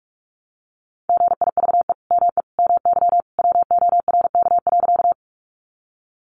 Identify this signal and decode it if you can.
Morse code, heard as a keyed tone, about 44 wpm, 700 Hz. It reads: GIVE ME MY WORK.
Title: Sentences from Top 200 Words 45wpm